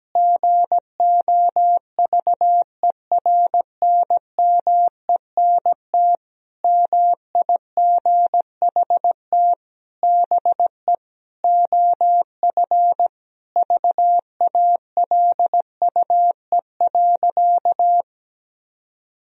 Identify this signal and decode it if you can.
Morse code, heard as a keyed tone, about 17 wpm, 700 Hz. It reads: GOVERNMENT MIGHT BE OF VALUE.
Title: Sentences from Top 1000 Words - Review Part 3 - 17wpm